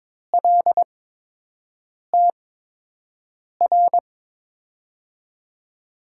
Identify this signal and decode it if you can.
Morse code, heard as a keyed tone, about 22 wpm, spaced as for 6 wpm, 700 Hz. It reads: LTR
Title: USA Call Signs - 6 Characters - 8x Character Spacing - 22wpm